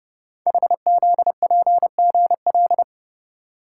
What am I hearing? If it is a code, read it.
Morse code, heard as a keyed tone, about 30 wpm, 700 Hz. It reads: HZPGL
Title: Code Groups - 5 Letters - 30wpm